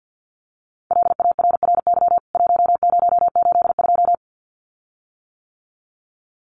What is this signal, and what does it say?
Morse code, heard as a keyed tone, about 50 wpm, 700 Hz. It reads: LARRY 1082
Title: CWT Exchange 50wpm